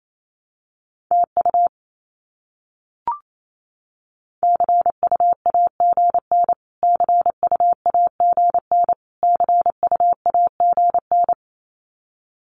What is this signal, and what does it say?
Morse code, heard as a keyed tone, about 28 wpm, 700 Hz. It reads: TU CUAGN CUAGN CUAGN
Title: Abbreviations Repeated 3x - 28wpm